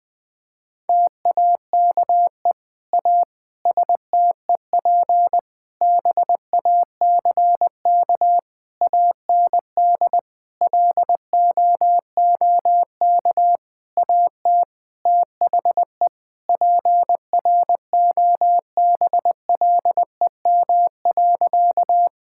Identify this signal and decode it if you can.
Morse code, heard as a keyed tone, about 20 wpm, 700 Hz. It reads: TAKE A STEP BACK AND LOOK AT THE PROBLEM.
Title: Sentences with 7-Word Phrases - 20wpm